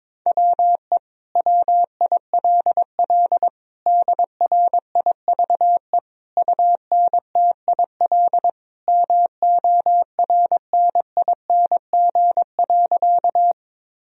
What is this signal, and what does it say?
Morse code, heard as a keyed tone, about 22 wpm, 700 Hz. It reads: WE WILL DRIVE UNTIL MORNING.
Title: Sentences from Top 1000 Words - Review Part 2 - 22wpm